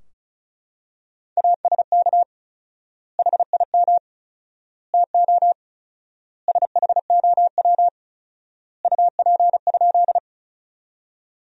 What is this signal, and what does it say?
Morse code, heard as a keyed tone, about 36 wpm, 700 Hz. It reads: ASK HIM TO SHOW UP?
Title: 4x Word-Spacing - Sentences from Top 200 Words - 35wpm